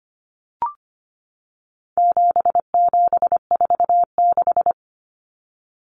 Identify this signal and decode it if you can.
Morse code, heard as a keyed tone, about 25 wpm, 700 Hz. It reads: 7746
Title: Numbers - 4-Digit 25wpm